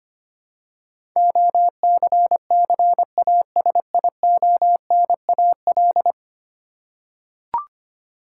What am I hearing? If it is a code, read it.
Morse code, heard as a keyed tone, about 25 wpm, 700 Hz. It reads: OCCASIONAL
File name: 10 Letter Words - First 3 Letters Given - 25wpm